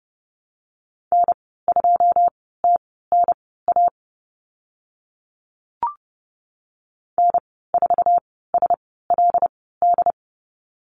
Morse code, heard as a keyed tone, about 30 wpm, spaced as for 17 wpm, 700 Hz. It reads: N2TNA N4SLD
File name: Lesson 14g - Number 2 - Call Signs 30wpm